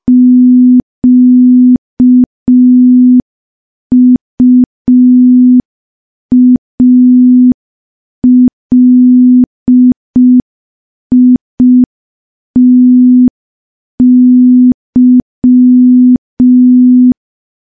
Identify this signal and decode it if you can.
Morse code, heard as a keyed tone, about 5 wpm, 250 Hz. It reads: QUALITY
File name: morse / morse-training-05WPM